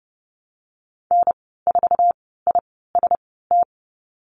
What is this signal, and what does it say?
Morse code, heard as a keyed tone, about 30 wpm, spaced as for 17 wpm, 690 Hz. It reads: N4IST
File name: Lesson 15g - Number 5 - Call Signs 30wpm